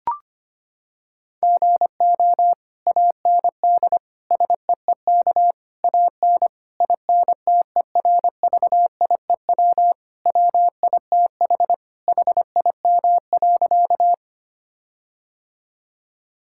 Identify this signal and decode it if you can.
Morse code, heard as a keyed tone, about 25 wpm, 700 Hz. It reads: GO AND SEEK AN INTERVIEW WITH HIM.
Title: Sentences from Top 1600 Words 25wpm